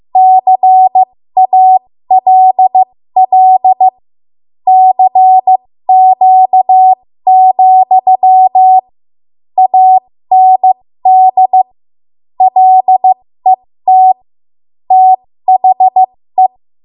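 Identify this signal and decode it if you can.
Morse code, heard as a keyed tone, about 15 wpm, spaced as for 13 wpm, 750 Hz. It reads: CALL CQ, AND LET THE
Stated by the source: 13 wpm Morse code practice archive transmission.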